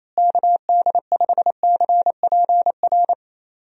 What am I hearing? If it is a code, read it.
Morse code, heard as a keyed tone, about 28 wpm, 700 Hz. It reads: KD5CPR
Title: USA Call Signs - 6 Characters - Repeated 2x - 28wpm